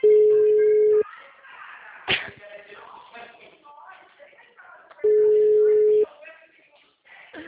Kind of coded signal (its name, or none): telephone call progress tone